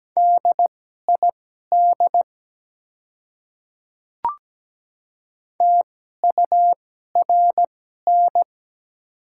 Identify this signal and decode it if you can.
Morse code, heard as a keyed tone, about 17 wpm, spaced as for 12 wpm, 700 Hz. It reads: DID TURN
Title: 2x Character Spacing - Top 200 Words - 17wpm